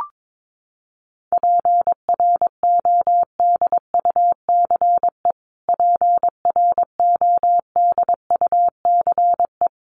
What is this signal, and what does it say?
Morse code, heard as a keyed tone, about 22 wpm, 700 Hz. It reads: PRODUCE PRODUCE
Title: Top 500 Words Repeated 22wpm